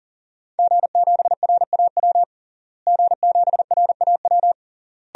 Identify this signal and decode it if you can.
Morse code, heard as a keyed tone, about 40 wpm, 700 Hz. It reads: G7RAW G7RAW
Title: DXCC-100 Call Signs - 5 Characters - Repeated 3x - 40wpm